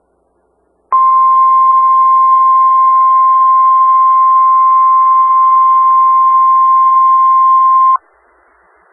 Emergency Alert System (SAME) tones (the 1050 Hz attention tone). It is heard from 0.9 to 8.0 s.
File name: Skywarn/2013/2013-03-18-Newton County WA4ASI Repeater